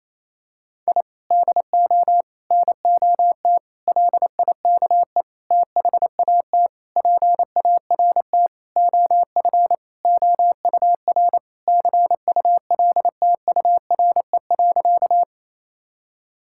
Morse code, heard as a keyed tone, about 28 wpm, 700 Hz. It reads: I DO NOT LIKE THAT PART OF OUR CULTURE.
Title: Sentences from Top 1000 Words - Review Part 6 - 28wpm